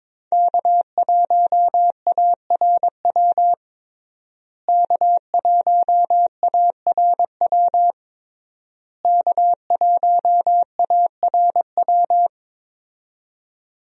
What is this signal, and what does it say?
Morse code, heard as a keyed tone, about 22 wpm, 700 Hz. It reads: K1ARW K1ARW K1ARW
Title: USA Call Signs - 5 Characters - Repeated 3x - 22wpm